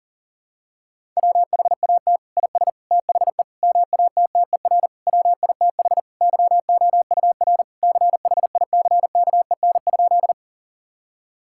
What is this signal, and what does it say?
Morse code, heard as a keyed tone, about 37 wpm, 700 Hz. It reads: WHAT IS THE MATTER WITH YOUR CHICKEN?
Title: Sentences from Top 1000 Words - Review Part 5 - 40wpm